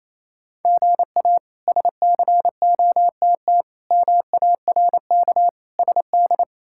Morse code, heard as a keyed tone, about 27 wpm, 700 Hz. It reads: GA SCOTT MARK SD
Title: K1USN SST QSO Practice - 28wpm